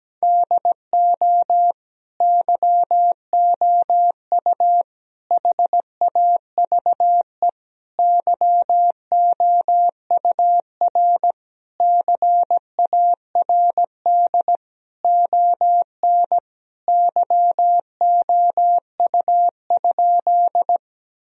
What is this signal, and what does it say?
Morse code, heard as a keyed tone, about 17 wpm, 700 Hz. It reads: DO YOU HAVE YOUR CARD ON YOU?
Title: Sentences from Top 1000 Words - Review Part 4 - 17wpm